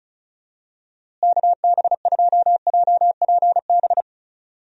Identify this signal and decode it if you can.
Morse code, heard as a keyed tone, about 36 wpm, 700 Hz. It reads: KB2JPB
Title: USA Call Signs - 6 Characters - Repeated 3x - 35wpm